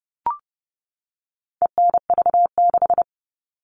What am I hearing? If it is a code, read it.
Morse code, heard as a keyed tone, about 30 wpm, 700 Hz. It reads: ENV6
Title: All Cut Numbers - 4-Digit 30wpm